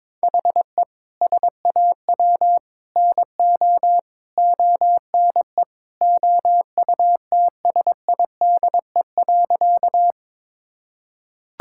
Morse code, heard as a keyed tone, about 22 wpm, 700 Hz. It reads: HE SAW NO ONE OUTSIDE.
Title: Sentences from Top 1000 Words - Review Part 6 - 22wpm